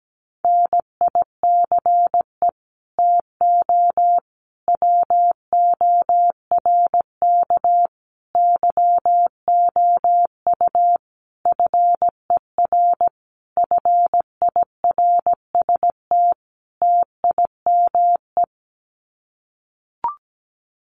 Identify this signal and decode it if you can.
Morse code, heard as a keyed tone, about 17 wpm, 700 Hz. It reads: NICE TO WORK YOU FER FIRST TIME
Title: Common QSO Phrases 17wpm